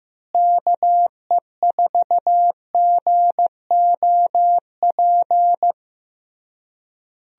Morse code, heard as a keyed tone, about 15 wpm, 700 Hz. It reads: KE4GOP
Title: USA Call Signs - 6 Characters - Repeated 3x - 15wpm